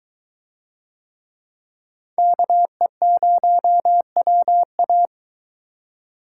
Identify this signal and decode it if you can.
Morse code, heard as a keyed tone, about 23 wpm, 700 Hz. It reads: KE0WA